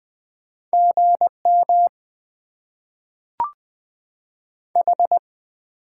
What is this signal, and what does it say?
Morse code, heard as a keyed tone, about 20 wpm, 700 Hz. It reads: GM H
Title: Top 90 QSO Elements - 8x Character Spacing - 20wpm